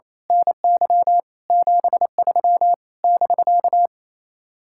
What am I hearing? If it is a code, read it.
Morse code, heard as a keyed tone, about 26 wpm, 700 Hz. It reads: NY 73 <BK>
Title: Speed-Racing Common QSO Phrases - 28wpm